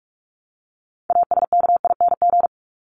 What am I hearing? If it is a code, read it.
Morse code, heard as a keyed tone, about 44 wpm, 700 Hz. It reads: ASKING
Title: ING Suffix - Top 100 Words with Base Words 45wpm